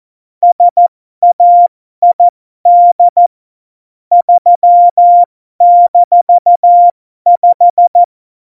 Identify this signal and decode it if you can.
Morse code, heard as a keyed tone, about 14 wpm, spaced as for 12 wpm, 700 Hz. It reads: SAID 3-5
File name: Morsekurs / Witze